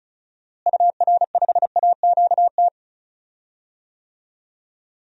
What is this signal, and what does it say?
Morse code, heard as a keyed tone, about 36 wpm, 700 Hz. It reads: UR5AQT
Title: DXCC-100 Call Signs - 6 Characters - Repeated 2x - 35wpm